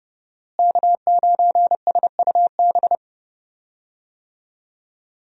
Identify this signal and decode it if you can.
Morse code, heard as a keyed tone, about 30 wpm, 700 Hz. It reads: K9SUB
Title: USA Call Signs - 5 Characters - Repeated 2x - 30wpm